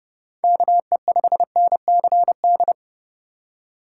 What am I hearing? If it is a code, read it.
Morse code, heard as a keyed tone, about 30 wpm, 700 Hz. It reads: KE5NCD
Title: USA Call Signs - 6 Characters - 2x Character Spacing - 30wpm